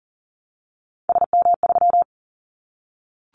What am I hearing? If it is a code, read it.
Morse code, heard as a keyed tone, about 38 wpm, 700 Hz. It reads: SM3
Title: Callsign Prefixes for Top 25 DXCC - Repeated 3x - 40wpm